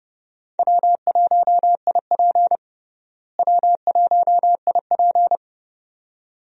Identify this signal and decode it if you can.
Morse code, heard as a keyed tone, about 30 wpm, 700 Hz. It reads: W1IP W1IP
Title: USA Call Signs - 4 Characters - Repeated 2x - 30wpm